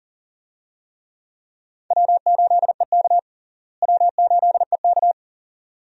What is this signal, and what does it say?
Morse code, heard as a keyed tone, about 37 wpm, 700 Hz. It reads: W8EK W8EK
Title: USA Call Signs - 4 Characters - Repeated 2x - 40wpm